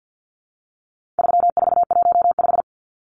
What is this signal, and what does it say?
Morse code, heard as a keyed tone, about 48 wpm, 700 Hz. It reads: 3415
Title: Numbers - 4-Digit 50wpm